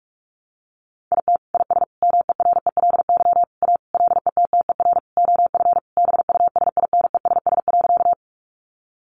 Morse code, heard as a keyed tone, about 44 wpm, 700 Hz. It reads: IT IS MERELY A LETTER OF BUSINESS.
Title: Sentences from Top 1400 Words 45wpm